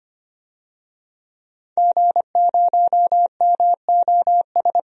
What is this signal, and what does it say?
Morse code, heard as a keyed tone, about 25 wpm, 700 Hz. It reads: G0MOS
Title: DXCC-100 Call Signs - 5 Characters - 25wpm